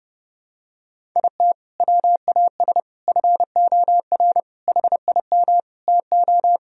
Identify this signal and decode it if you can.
Morse code, heard as a keyed tone, about 30 wpm, 700 Hz. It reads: IT WAS FOR HIM TO
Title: Sentences from Top 1000 Words - Review Part 1 - 30wpm